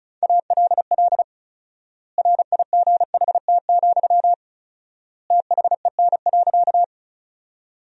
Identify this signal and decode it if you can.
Morse code, heard as a keyed tone, about 34 wpm, 700 Hz. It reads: ALL RIGHT, THEN.
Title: 4x Word-Spacing - Sentences from Top 200 Words - 35wpm